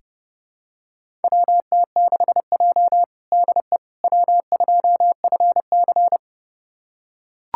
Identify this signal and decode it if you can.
Morse code, heard as a keyed tone, about 30 wpm, 700 Hz. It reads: WT6J DE W2FC
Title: USA Callsign DE Callsign - 4 Characters - 30wpm